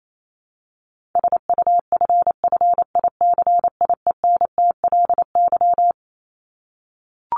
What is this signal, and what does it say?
Morse code, heard as a keyed tone, about 28 wpm, 700 Hz. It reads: SUFFICIENTLY
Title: LY Suffix - Top 100 Words with Base Words 28wpm